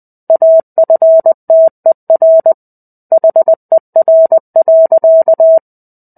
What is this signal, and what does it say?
Morse code, heard as a keyed tone, about 20 wpm, 650 Hz. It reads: AFTER HER.
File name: 05-Chapter-05 20 WPM